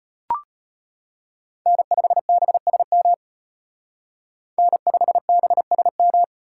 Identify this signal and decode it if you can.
Morse code, heard as a keyed tone, about 30 wpm, 700 Hz. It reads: N5BSM N5BSM
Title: Speed Racing - USA 1x3 Call Signs 25wpm